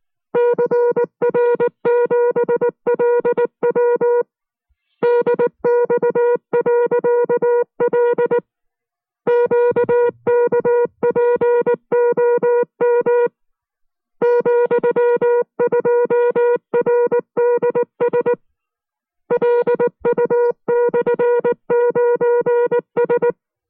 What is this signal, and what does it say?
Morse code, heard as a keyed tone, about 19 wpm, 480 Hz. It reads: CR7LW DX.L QKPOM ,2RDS LU/9S